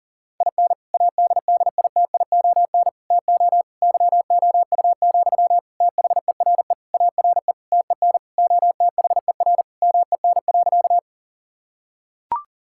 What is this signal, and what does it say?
Morse code, heard as a keyed tone, about 37 wpm, 690 Hz. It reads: IN ADDITION TO YOU, THERE ARE TEN OTHER MEN.
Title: Sentences from Top 1000 Words - Review Part 6 - 40wpm